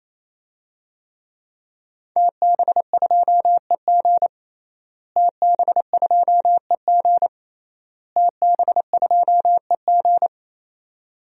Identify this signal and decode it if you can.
Morse code, heard as a keyed tone, about 27 wpm, 700 Hz. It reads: TB2EG TB2EG TB2EG
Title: DXCC-100 Call Signs - 5 Characters - Repeated 3x - 28wpm